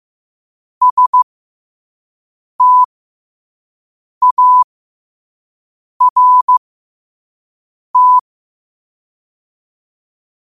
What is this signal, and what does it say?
Morse code, heard as a keyed tone, about 15 wpm, spaced as for 5 wpm, 1000 Hz. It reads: START
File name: qotdfiles / 5wpm